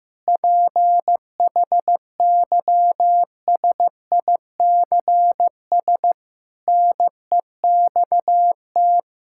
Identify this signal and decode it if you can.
Morse code, heard as a keyed tone, about 15 wpm, 700 Hz. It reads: PHYSICS NEXT
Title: Sentences from Top 1000 Words - Review Part 5 - 15wpm